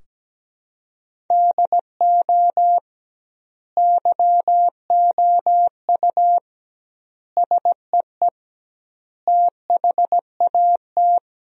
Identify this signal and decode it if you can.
Morse code, heard as a keyed tone, about 17 wpm, 700 Hz. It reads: DO YOU SEE THAT
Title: 2x Word-Spacing - Sentences from Top 500 Words - 17wpm